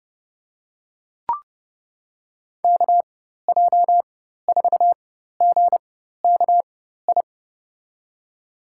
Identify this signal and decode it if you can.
Morse code, heard as a keyed tone, about 30 wpm, spaced as for 14 wpm, 700 Hz. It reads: KJ4GKI
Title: USA Call Signs - 6 Characters - 4x Character Spacing - 30wpm